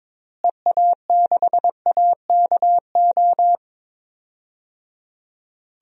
Morse code, heard as a keyed tone, about 22 wpm, 700 Hz. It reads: EA6AKO
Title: DXCC-100 Call Signs - 6 Characters - Repeated 3x - 22wpm